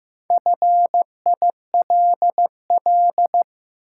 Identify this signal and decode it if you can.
Morse code, heard as a keyed tone, about 15 wpm, 700 Hz. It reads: FILL
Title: Sentences from Top 1000 Words - Review Part 3 - New Word Drill - 15wpm